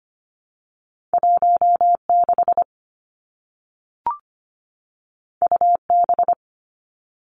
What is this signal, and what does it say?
Morse code, heard as a keyed tone, about 25 wpm, 700 Hz. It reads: 16 UB
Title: Two Challenging Letter-Number Combinations 25wpm